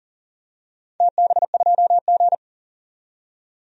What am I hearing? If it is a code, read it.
Morse code, heard as a keyed tone, about 40 wpm, 700 Hz. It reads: TB2G
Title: DXCC-100 Call Signs - 4 Characters - 40wpm